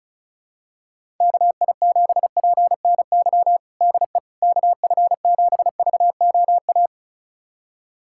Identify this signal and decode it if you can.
Morse code, heard as a keyed tone, about 34 wpm, 700 Hz. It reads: KI7PNY DE KF7VOA
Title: USA Callsign DE Callsign - 6 Characters - 35wpm